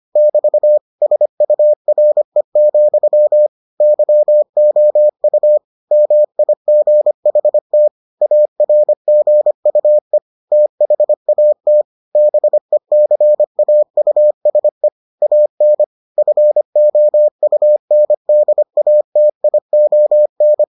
Morse code, heard as a keyed tone, about 25 wpm, 600 Hz. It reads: <BT> SURE, YOU MIGHT ARGUE THAT BECAUSE AN FOUNDATION